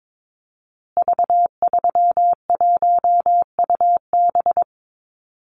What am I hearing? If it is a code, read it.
Morse code, heard as a keyed tone, about 22 wpm, 700 Hz. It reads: V31UB
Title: Call Signs Repeated 2x - 22wpm